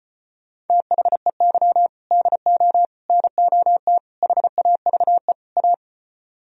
Morse code, heard as a keyed tone, about 33 wpm, 700 Hz. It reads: THEY DO NOT HAVE A